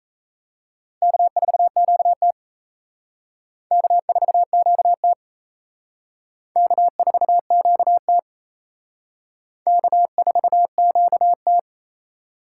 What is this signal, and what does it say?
Morse code, heard as a keyed tone, about 35 wpm, 700 Hz. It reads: K4QT K4QT K4QT K4QT
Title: Speed Racing - USA 1x2 Call Signs 28wpm